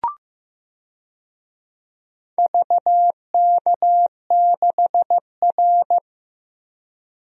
Morse code, heard as a keyed tone, about 15 wpm, 700 Hz. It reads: VK6R
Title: DXCC-100 Call Signs - 4 Characters - Repeated 2x - 15wpm